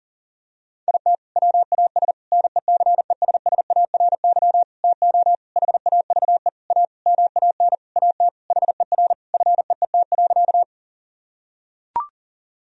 Morse code, heard as a keyed tone, about 40 wpm, 700 Hz. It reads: IT WAS NECESSARY TO HAVE A MAN AT HER FEET.
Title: Sentences from Top 1000 Words - Review Part 1 - 40wpm